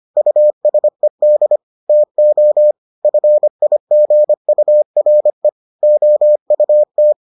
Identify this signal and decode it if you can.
Morse code, heard as a keyed tone, about 25 wpm, 600 Hz. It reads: USED TO FIGURE OUT